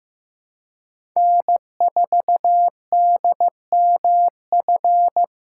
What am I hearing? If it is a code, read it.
Morse code, heard as a keyed tone, about 15 wpm, 700 Hz. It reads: N4DMF
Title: USA Call Signs - 5 Characters - 2x Character Spacing - 15wpm